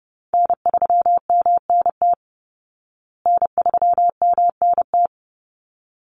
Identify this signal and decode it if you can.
Morse code, heard as a keyed tone, about 30 wpm, 700 Hz. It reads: N3MNT N3MNT
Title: Call Signs Repeated 3x - 30wpm